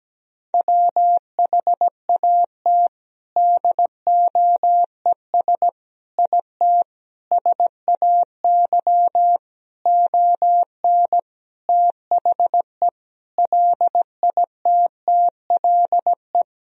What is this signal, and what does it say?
Morse code, heard as a keyed tone, about 17 wpm, 700 Hz. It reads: WHAT DOES IT SAY ON THE LITTLE
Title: Sentences from Top 1000 Words - Review Part 2 - 17wpm